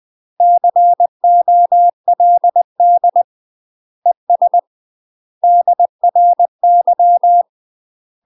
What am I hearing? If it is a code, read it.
Morse code, heard as a keyed tone, about 20 wpm, 700 Hz. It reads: COLD ES DRY
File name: UK MASTER QSOs TRAINING / 20WPM / UK MASTER QSOs 0047